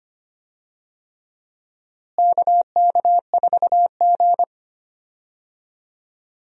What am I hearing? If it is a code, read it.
Morse code, heard as a keyed tone, about 25 wpm, 700 Hz. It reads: KK4G
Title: USA Call Signs - 4 Characters - Repeated 3x - 25wpm